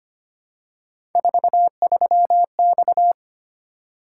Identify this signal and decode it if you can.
Morse code, heard as a keyed tone, about 25 wpm, 700 Hz. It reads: 43X
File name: Three Letter-Number Combinations - 25wpm